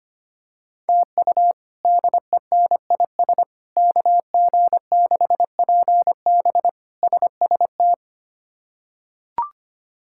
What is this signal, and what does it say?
Morse code, heard as a keyed tone, about 25 wpm, 700 Hz. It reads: TU DENIS KG6PB SST
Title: K1USN SST QSO Practice - 25wpm